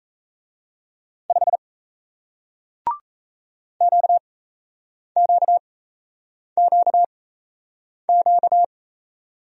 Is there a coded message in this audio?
Yes: Morse code from 1.3 to 8.6 s, heard as a keyed tone, about 38 wpm, spaced as for 8 wpm, 700 Hz. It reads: 5 QQQQ